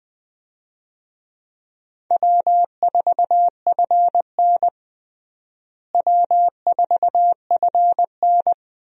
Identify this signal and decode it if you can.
Morse code, heard as a keyed tone, about 20 wpm, 700 Hz. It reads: W4FN W4FN
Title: USA Call Signs - 4 Characters - Repeated 2x - 20wpm